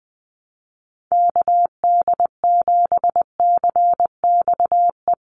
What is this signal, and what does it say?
Morse code, heard as a keyed tone, about 20 wpm, 700 Hz. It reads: KD7CXE
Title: USA Call Signs 20wpm